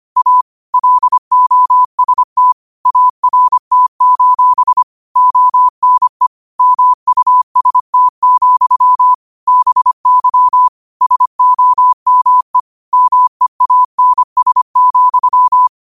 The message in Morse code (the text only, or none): A LOST ART: ONE MUST, BY SOME MEANS,